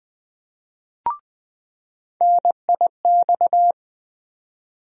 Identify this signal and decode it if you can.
Morse code, heard as a keyed tone, about 20 wpm, 700 Hz. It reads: NIX
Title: USA Call Sign Postfixes - 3 Letters 20wpm